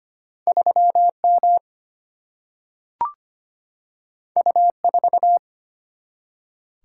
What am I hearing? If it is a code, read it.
Morse code, heard as a keyed tone, about 25 wpm, 700 Hz. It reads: 3 M U 4
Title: Single-Letter Single-Number 25wpm